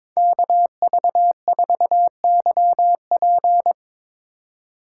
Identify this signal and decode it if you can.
Morse code, heard as a keyed tone, about 22 wpm, 700 Hz. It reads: KV4YP
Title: Speed Racing - USA Call Signs 22wpm